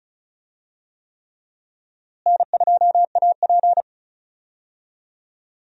Morse code, heard as a keyed tone, about 34 wpm, 700 Hz. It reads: N2AP
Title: USA Call Signs - 4 Characters - Repeated 3x - 35wpm